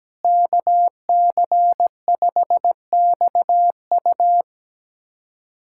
Morse code, heard as a keyed tone, about 17 wpm, 700 Hz. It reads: KC5XU
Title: USA Callsign DE Callsign - 5 Characters - 17wpm